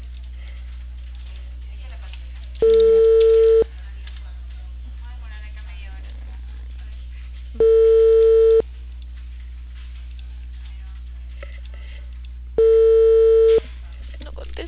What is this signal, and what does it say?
Ringback tone (China, 450 Hz), with speech.